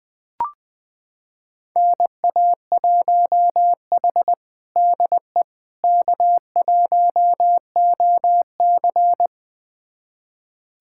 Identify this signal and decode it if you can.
Morse code, heard as a keyed tone, about 21 wpm, 700 Hz. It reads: NA1H DE K1OC
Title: USA Callsign DE Callsign - 4 Characters - 20wpm